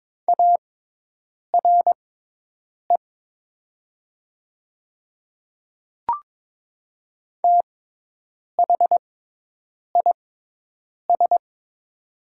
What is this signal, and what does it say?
Morse code, heard as a keyed tone, about 22 wpm, spaced as for 8 wpm, 700 Hz. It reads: ARE THIS
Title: 6x Character Spacing - Top 300 Words - 22wpm